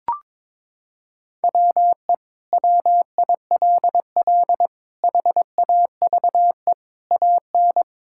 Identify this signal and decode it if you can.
Morse code, heard as a keyed tone, about 22 wpm, 700 Hz. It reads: WE WILL HAVE AN
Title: Sentences from Top 1000 Words - Review Part 5 - 22wpm